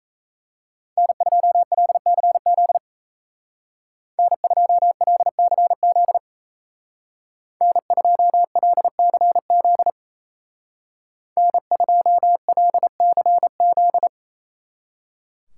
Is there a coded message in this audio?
Yes: Morse code from 1.0 to 14.1 s, heard as a keyed tone, about 37 wpm, 700 Hz. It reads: N2LCZ N2LCZ N2LCZ N2LCZ